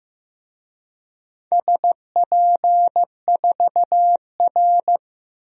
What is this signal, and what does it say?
Morse code, heard as a keyed tone, about 15 wpm, 700 Hz. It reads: SP4R
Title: DXCC-100 Call Signs - 4 Characters - Repeated 2x - 15wpm